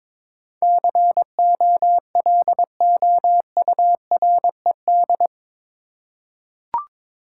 Morse code, heard as a keyed tone, about 21 wpm, 700 Hz. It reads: COLOURED
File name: practice / 22wpm / Eight Letter Words - 22wpm